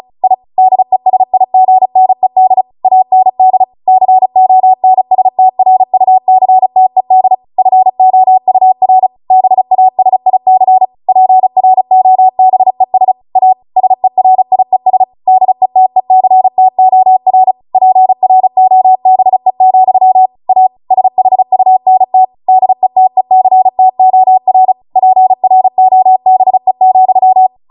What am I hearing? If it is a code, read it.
Morse code, heard as a keyed tone, about 36 wpm, 750 Hz. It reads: I DESIGNED AND CONSTRUCTED FOUR BASIC PROBES A SERIES DETECTOR PROBE, A SHUNT DETECTOR PROBE,